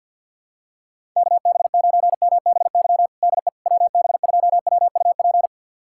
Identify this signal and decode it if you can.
Morse code, heard as a keyed tone, about 52 wpm, 700 Hz. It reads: KB9MBY DE WB2WUP
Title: USA Callsign DE Callsign - 6 Characters - 50wpm